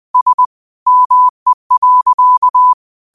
Morse code, heard as a keyed tone, about 20 wpm, 1000 Hz. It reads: S ME.
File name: qotdfiles / 20wpm